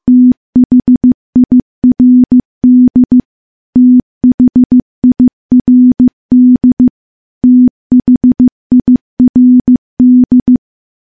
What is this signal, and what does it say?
Morse code, heard as a keyed tone, about 15 wpm, 250 Hz. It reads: THIRD THIRD THIRD
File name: morse-training-15WPM